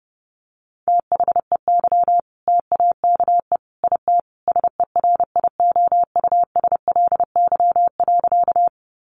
Morse code, heard as a keyed tone, about 30 wpm, 700 Hz. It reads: THEY TAKE IT SERIOUSLY.
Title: Sentences from Top 1200 Words 30wpm